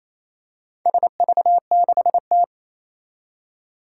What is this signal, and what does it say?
Morse code, heard as a keyed tone, about 27 wpm, 700 Hz. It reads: SV6T
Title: Four Challenging Letter-Number Combinations 28wpm